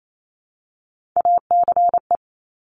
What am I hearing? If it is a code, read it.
Morse code, heard as a keyed tone, about 28 wpm, 700 Hz. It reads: ACE
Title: Top Three Letter Combinations in Words 28wpm